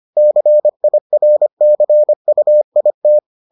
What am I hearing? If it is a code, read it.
Morse code, heard as a keyed tone, about 25 wpm, 600 Hz. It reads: CIRCUIT